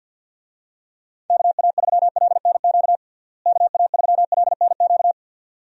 Morse code, heard as a keyed tone, about 53 wpm, 700 Hz. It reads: KA3LNQ KA3LNQ
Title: USA Call Signs - 6 Characters - Repeated 2x - 50wpm